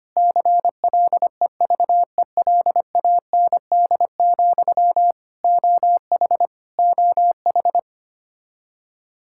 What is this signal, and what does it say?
Morse code, heard as a keyed tone, about 25 wpm, 700 Hz. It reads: CLEVELAND, OH OH
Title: Practice QSO - Exchange 1 - Beginning - 25wpm